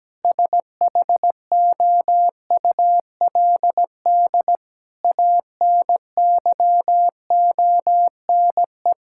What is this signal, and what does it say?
Morse code, heard as a keyed tone, about 17 wpm, 700 Hz. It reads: SHOULD ANYONE
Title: Sentences from Top 1600 Words 17wpm